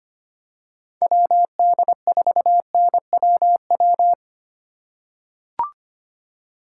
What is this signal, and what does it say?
Morse code, heard as a keyed tone, about 25 wpm, 700 Hz. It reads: WD4NWW
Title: USA Call Signs - 6 Characters - 3x Character Spacing - 25wpm